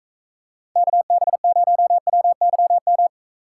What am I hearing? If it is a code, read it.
Morse code, heard as a keyed tone, about 35 wpm, 700 Hz. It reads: KB0WYM
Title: Speed Racing - USA 2x3 Call Signs 28wpm